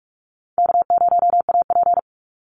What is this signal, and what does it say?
Morse code, heard as a keyed tone, about 46 wpm, 700 Hz. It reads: K0AP
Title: Easy Call Signs - 45wpm